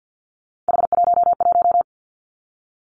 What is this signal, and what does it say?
Morse code, heard as a keyed tone, about 50 wpm, 700 Hz. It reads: H11
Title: Three Challenging Letter-Number Combinations 50wpm